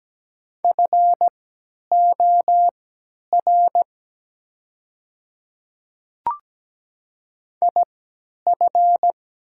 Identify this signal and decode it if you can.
Morse code, heard as a keyed tone, about 17 wpm, spaced as for 10 wpm, 700 Hz. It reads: FOR IF
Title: 3x Character Spacing - Top 100 Words - 17wpm